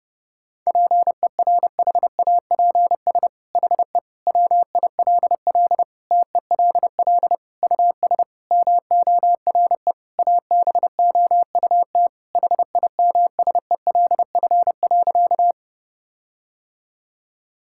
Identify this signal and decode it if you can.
Morse code, heard as a keyed tone, about 32 wpm, 700 Hz. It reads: PERHAPS HE WILL TELL US MORE ABOUT HIMSELF.
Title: Sentences from Top 1600 Words 30wpm